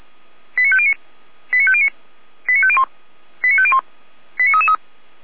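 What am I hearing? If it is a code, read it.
Five-tone selective call (ZVEI), heard as quick runs of five tones, tones of about 70 ms. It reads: 89509 89509 89501 89501 89333
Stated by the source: ZVEI five-tone selective call received by SDR.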